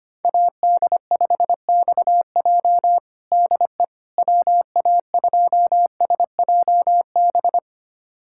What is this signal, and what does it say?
Morse code, heard as a keyed tone, about 25 wpm, 700 Hz. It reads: AD5XJ DE WA2SJB
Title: USA Callsign DE Callsign 25wpm